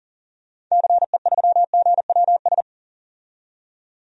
Morse code, heard as a keyed tone, about 40 wpm, 700 Hz. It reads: CE3GWS
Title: DXCC-100 Call Signs - 6 Characters - 40wpm